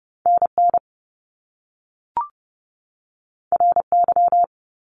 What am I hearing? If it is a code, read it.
Morse code, heard as a keyed tone, about 30 wpm, 700 Hz. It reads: NN RY
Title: Top Two Letter Combinations in Words 30wpm